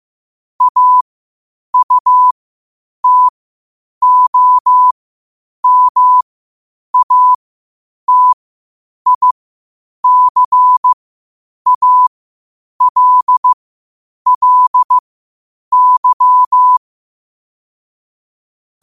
Morse code, heard as a keyed tone, about 15 wpm, spaced as for 8 wpm, 1000 Hz. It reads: AUTOMATICALLY
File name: qotdfiles / 8wpm